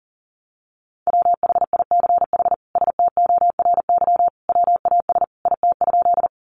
Morse code, heard as a keyed tone, about 40 wpm, 700 Hz. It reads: WHICH STORY WAS IT?
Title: Sentences from Top 300 Words 40wpm